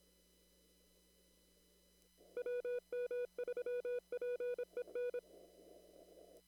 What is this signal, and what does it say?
Morse code, heard as a keyed tone, about 26 wpm, 490 Hz. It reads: WM3PEN
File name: CW / kt5j wpx cw-op-k5pi